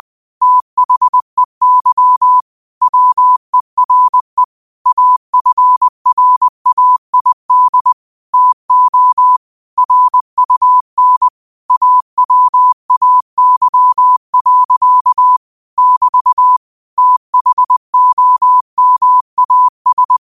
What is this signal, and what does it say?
Morse code, heard as a keyed tone, about 20 wpm, 1000 Hz. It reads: THEY WERE AFRAID TO RUN AWAY. <BT> THOMAS
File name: qotdfiles / 20wpm